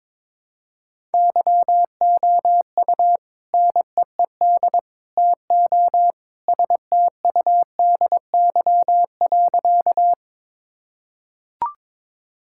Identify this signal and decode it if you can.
Morse code, heard as a keyed tone, about 22 wpm, 700 Hz. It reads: YOU NEED TO STUDY.
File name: practice / 22wpm / Sentences with 3-Word Phrases - 22wpm